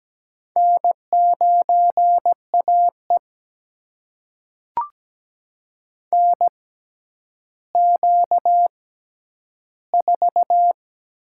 Morse code, heard as a keyed tone, about 17 wpm, spaced as for 10 wpm, 700 Hz. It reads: N9AE NQ4
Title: USA Call Signs - 4 Characters - 6x Character Spacing - 17wpm